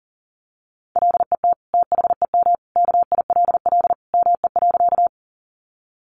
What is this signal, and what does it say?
Morse code, heard as a keyed tone, about 40 wpm, 700 Hz. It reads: LET THEM KILL ME.
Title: Sentences from Top 1400 Words 40wpm